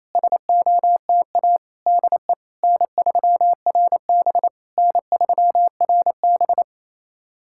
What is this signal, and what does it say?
Morse code, heard as a keyed tone, about 28 wpm, 690 Hz. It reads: SOTA DE N3RB N3RB
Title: SOTA QSO Practice 28wpm